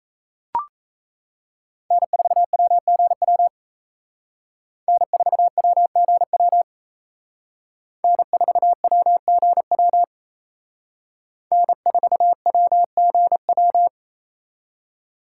Morse code, heard as a keyed tone, about 34 wpm, 700 Hz. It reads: N4WGW N4WGW N4WGW N4WGW
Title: Speed Racing - USA 1x3 Call Signs 28wpm